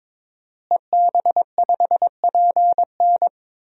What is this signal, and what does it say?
Morse code, heard as a keyed tone, about 21 wpm, 700 Hz. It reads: EB5PN
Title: DXCC-100 Call Signs - 5 Characters - Repeated 2x - 22wpm